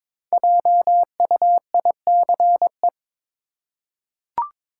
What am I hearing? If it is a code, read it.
Morse code, heard as a keyed tone, about 22 wpm, 700 Hz. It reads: JUICE
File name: 5 Letter Words - First 3 Letters Given - 22wpm